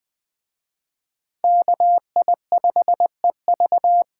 Morse code, heard as a keyed tone, about 20 wpm, 700 Hz. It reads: KI5EV